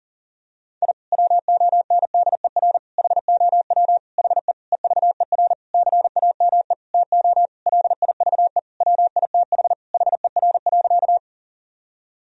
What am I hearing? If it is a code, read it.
Morse code, heard as a keyed tone, about 40 wpm, 700 Hz. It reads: I WONDER HOW HE EVER CAME TO LIVE WITH HER.
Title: Sentences from Top 1000 Words - Review Part 3 - 40wpm